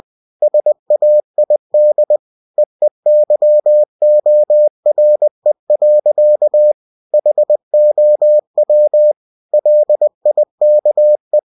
Morse code, heard as a keyed tone, about 20 wpm, 600 Hz. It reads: SAID EEYORE. HOW LIKE
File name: cw-books / winnie-the-pooh / winnie-20wpm-page-0027